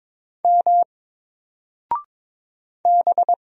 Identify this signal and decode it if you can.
Morse code, heard as a keyed tone, about 22 wpm, 700 Hz. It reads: M B
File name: Single-Letter-Number-22wpm